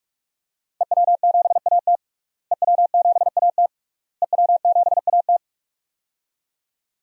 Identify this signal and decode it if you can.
Morse code, heard as a keyed tone, about 41 wpm, 700 Hz. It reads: EW7AT EW7AT EW7AT
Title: DXCC-100 Call Signs - 5 Characters - Repeated 3x - 45wpm